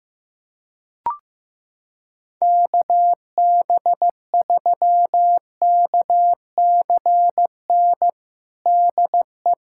Morse code, heard as a keyed tone, about 15 wpm, 700 Hz. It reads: KB3KCN DE
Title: USA Callsign DE Callsign - 6 Characters - 15wpm